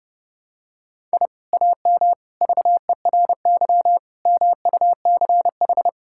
Morse code, heard as a keyed tone, about 30 wpm, 700 Hz. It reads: I AM VERY MUCH
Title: Sentences from Top 1000 Words - Review Part 6 - 30wpm